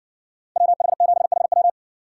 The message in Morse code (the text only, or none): WH7HW